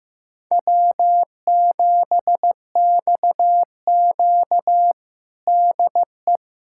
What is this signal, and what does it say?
Morse code, heard as a keyed tone, about 15 wpm, 700 Hz. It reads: W7XQ DE
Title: USA Callsign DE Callsign - 4 Characters - 15wpm